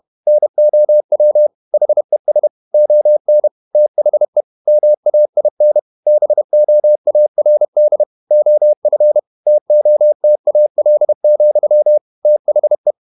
Morse code, heard as a keyed tone, about 31 wpm, 600 Hz. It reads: NOW HES ON THE MAIN BOARD OF TOTAL, THE
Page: Never 31wpm | CW med Gnister